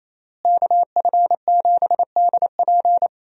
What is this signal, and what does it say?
Morse code, heard as a keyed tone, about 28 wpm, 700 Hz. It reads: KF7DP
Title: USA Call Signs - 5 Characters - Repeated 2x - 28wpm